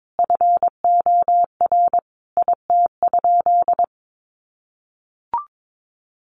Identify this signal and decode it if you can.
Morse code, heard as a keyed tone, about 22 wpm, 700 Hz. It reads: FOR IT?
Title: Sentences from Top 1200 Words 22wpm